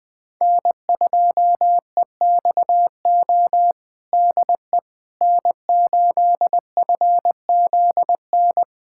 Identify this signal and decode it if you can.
Morse code, heard as a keyed tone, about 20 wpm, 700 Hz. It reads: N2EXO DE N8FZN
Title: USA Callsign DE Callsign - 5 Characters - 20wpm